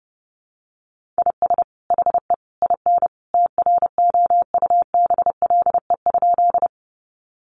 Morse code, heard as a keyed tone, about 28 wpm, 700 Hz. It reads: IS HE IN TROUBLE?
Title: Sentences from Top 1200 Words 30wpm